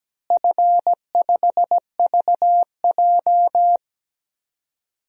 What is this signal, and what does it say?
Morse code, heard as a keyed tone, about 17 wpm, 700 Hz. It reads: F5VJ
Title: DXCC-100 Call Signs - 4 Characters - Repeated 2x - 17wpm